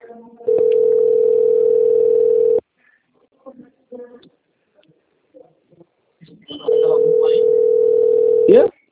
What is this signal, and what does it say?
Ringback tone (North America), 440 + 480 Hz, with speech.
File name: out- - - - - .38701